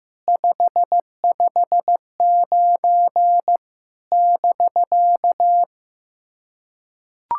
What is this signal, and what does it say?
Morse code, heard as a keyed tone, about 15 wpm, 700 Hz. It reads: 559 <BK>
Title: SOTA QSO Practice 15wpm